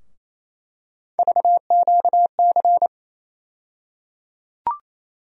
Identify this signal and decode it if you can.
Morse code, heard as a keyed tone, about 28 wpm, 700 Hz. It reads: VQC
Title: USA Call Sign Postfixes - 3 Letters 28wpm